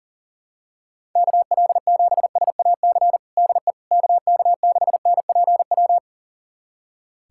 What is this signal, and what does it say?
Morse code, heard as a keyed tone, about 37 wpm, 700 Hz. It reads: KL7SAC DE KK6NPW
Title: USA Callsign DE Callsign - 6 Characters - 40wpm